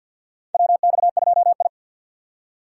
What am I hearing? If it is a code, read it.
Morse code, heard as a keyed tone, about 49 wpm, 700 Hz. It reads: WX2I
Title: USA Call Signs - 4 Characters - 3x Character Spacing - 50wpm